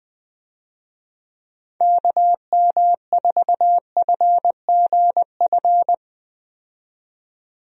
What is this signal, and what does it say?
Morse code, heard as a keyed tone, about 20 wpm, 700 Hz. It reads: KM4FGF
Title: USA Call Signs - 6 Characters - 20wpm